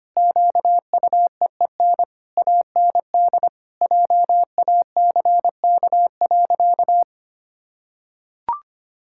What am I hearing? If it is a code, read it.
Morse code, heard as a keyed tone, about 25 wpm, 700 Hz. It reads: QUEEN AND JACK.
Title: Pangram Challenge v2 - 25wpm